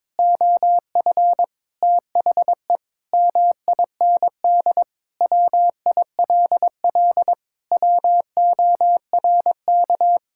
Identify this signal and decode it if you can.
Morse code, heard as a keyed tone, about 22 wpm, 700 Hz. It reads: OF THE MIND WILL WORK
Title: Sentences from Top 1000 Words - Review Part 5 - 22wpm